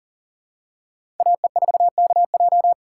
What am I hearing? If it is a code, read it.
Morse code, heard as a keyed tone, about 38 wpm, 700 Hz. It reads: AE4KJ
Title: USA Call Signs - 5 Characters - 4x Character Spacing - 40wpm